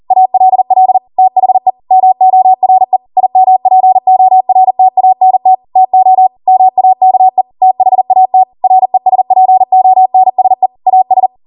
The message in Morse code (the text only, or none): ALL THE MORE IMPORTANT TO MAKE THAT RESPONSE AS